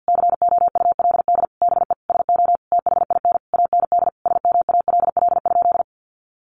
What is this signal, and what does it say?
Morse code, heard as a keyed tone, about 50 wpm, 700 Hz. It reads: COULD BE SO THIN AND SMALL?
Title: Sentences from Top 1200 Words 50wpm